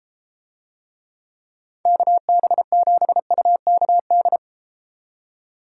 Morse code, heard as a keyed tone, about 30 wpm, 700 Hz. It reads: KB7UKD